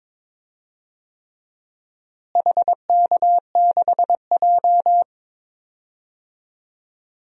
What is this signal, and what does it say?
Morse code, heard as a keyed tone, about 22 wpm, 700 Hz. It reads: HK6J
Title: DXCC-100 Call Signs - 4 Characters - Repeated 3x - 22wpm